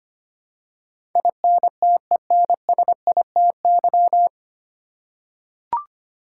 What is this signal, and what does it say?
Morse code, heard as a keyed tone, about 25 wpm, 700 Hz. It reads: INTENSITY
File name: practice / 25wpm / 9 Letter Words - First 3 Letters Given - 25wpm